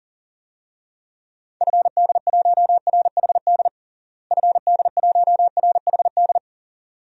Morse code, heard as a keyed tone, about 40 wpm, 700 Hz. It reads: FD1RHD FD1RHD
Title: DXCC-100 Call Signs - 6 Characters - Repeated 2x - 40wpm